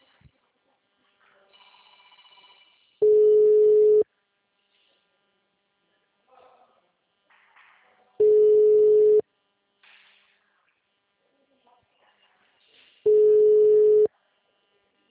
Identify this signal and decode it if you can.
Ringback tone (Europe, 425 Hz).